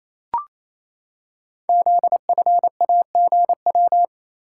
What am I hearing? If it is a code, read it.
Morse code, heard as a keyed tone, about 29 wpm, 700 Hz. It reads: ZFAGW
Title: Code Groups - 5 Letters - 28wpm